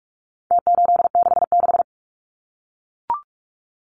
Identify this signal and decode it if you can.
Morse code, heard as a keyed tone, about 44 wpm, 700 Hz. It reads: T866
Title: Numbers - Contest Style - 4-Digit 45wpm